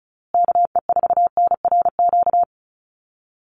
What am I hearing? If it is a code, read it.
Morse code, heard as a keyed tone, about 34 wpm, 700 Hz. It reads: KE4NRQ
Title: USA Call Signs Repeated 2x 35wpm